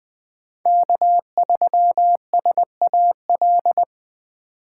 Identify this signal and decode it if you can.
Morse code, heard as a keyed tone, about 20 wpm, 700 Hz. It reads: K3SAL